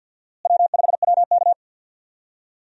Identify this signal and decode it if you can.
Morse code, heard as a keyed tone, about 51 wpm, 700 Hz. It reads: W5WK
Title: USA Call Signs - 4 Characters - Repeated 2x - 50wpm